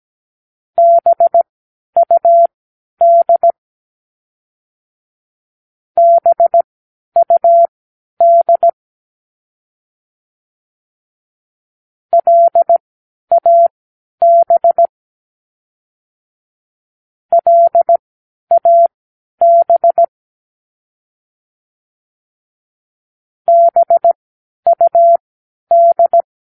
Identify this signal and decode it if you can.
Morse code, heard as a keyed tone, about 17 wpm, spaced as for 10 wpm, 680 Hz. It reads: BUD BUD LAB LAB BUD
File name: k7qo / Practicewords